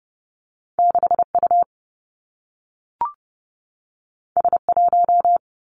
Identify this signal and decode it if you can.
Morse code, heard as a keyed tone, about 30 wpm, 700 Hz. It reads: 6 U S 1
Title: Two Challenging Letter-Number Combinations 30wpm